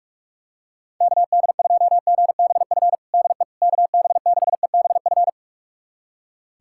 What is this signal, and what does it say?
Morse code, heard as a keyed tone, about 46 wpm, 700 Hz. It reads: KD2GBF DE KB6EBF
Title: USA Callsign DE Callsign 45wpm